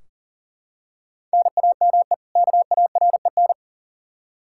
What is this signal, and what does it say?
Morse code, heard as a keyed tone, about 40 wpm, 700 Hz. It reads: NAME KAREN
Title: Practice QSO - Exchange 1 - Beginning - 40wpm